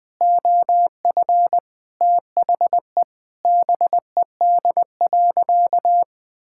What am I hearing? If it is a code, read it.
Morse code, heard as a keyed tone, about 20 wpm, 700 Hz. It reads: OF THE BED.
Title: Sentences with 3-Word Phrases - 20wpm